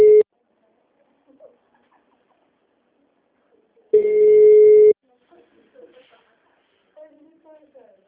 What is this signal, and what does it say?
Ringback tone (Europe, 425 Hz), with speech.